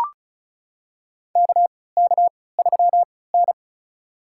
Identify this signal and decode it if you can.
Morse code, heard as a keyed tone, about 36 wpm, spaced as for 20 wpm, 700 Hz. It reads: KK3N
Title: USA Call Signs - 4 Characters - 3x Character Spacing - 35wpm